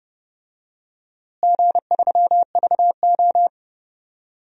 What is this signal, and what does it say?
Morse code, heard as a keyed tone, about 32 wpm, 700 Hz. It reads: G3VO
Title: DXCC-100 Call Signs - 4 Characters - Repeated 2x - 30wpm